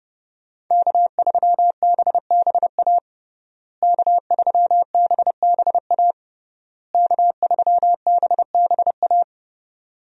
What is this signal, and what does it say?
Morse code, heard as a keyed tone, about 32 wpm, 700 Hz. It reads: K3BBA K3BBA K3BBA
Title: USA Call Signs - 5 Characters - Repeated 3x - 30wpm